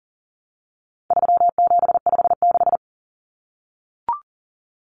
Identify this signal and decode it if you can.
Morse code, heard as a keyed tone, about 40 wpm, 700 Hz. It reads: 3756
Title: Common Cut Numbers - 4-Digit 40wpm